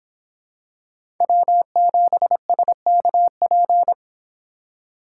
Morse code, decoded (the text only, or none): W7SKP